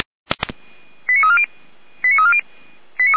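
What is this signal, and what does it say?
Five-tone selective call (ZVEI), heard as quick runs of five tones, tones of about 80 ms. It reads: 89340 89340
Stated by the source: ZVEI five-tone call received on an SDR, fire brigade Orlamünde (feuerwehr-orlamuende).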